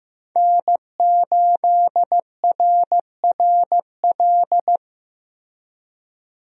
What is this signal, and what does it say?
Morse code, heard as a keyed tone, about 15 wpm, 700 Hz. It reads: N8RRL
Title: USA Call Signs - 5 Characters - Repeated 2x - 15wpm